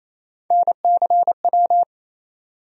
Morse code, heard as a keyed tone, about 27 wpm, 700 Hz. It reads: NCW
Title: USA Call Sign Postfixes - 3 Letters 28wpm